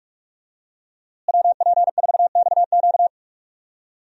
Morse code, heard as a keyed tone, about 42 wpm, 700 Hz. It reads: WP4XQ
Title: USA Call Signs - 5 Characters - Repeated 2x - 45wpm